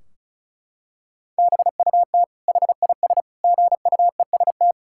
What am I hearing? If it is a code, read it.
Morse code, heard as a keyed tone, about 36 wpm, 700 Hz. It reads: BUT HIS GUEST
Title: Sentences from Top 1600 Words 35wpm